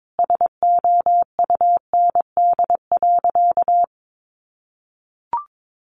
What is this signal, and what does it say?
Morse code, heard as a keyed tone, about 22 wpm, 700 Hz. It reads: SOUND.
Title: A Sound Of Thunder 22wpm